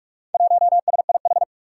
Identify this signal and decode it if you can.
Morse code, heard as a keyed tone, about 46 wpm, 700 Hz. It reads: 1SIH
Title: Four Challenging Letter-Number Combinations 45wpm